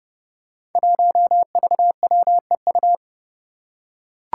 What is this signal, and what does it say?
Morse code, heard as a keyed tone, about 32 wpm, 700 Hz. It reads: 1VWEU
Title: Five Challenging Letter-Number Combinations 30wpm